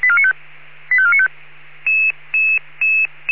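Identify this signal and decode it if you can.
Five-tone selective call (ZVEI), heard as quick runs of five tones, tones of about 80 ms. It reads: 85495
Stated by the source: ZVEI five-tone call 85492 received by SDR.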